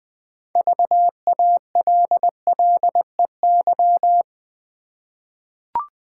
Morse code, heard as a keyed tone, about 20 wpm, 700 Hz. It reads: VALLEY
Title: 6 Letter Words - First 3 Letters Given - 20wpm